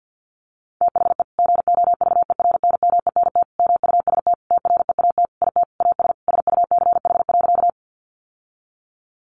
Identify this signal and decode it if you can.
Morse code, heard as a keyed tone, about 50 wpm, 700 Hz. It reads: THE GOVERNMENT MUST TREAT IT AS SUCH.
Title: Sentences from Top 1300 Words 50wpm